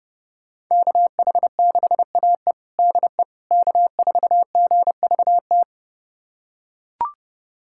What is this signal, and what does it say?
Morse code, heard as a keyed tone, about 32 wpm, 700 Hz. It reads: KH6AE DE K4GVT
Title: USA Callsign DE Callsign - 5 Characters - 30wpm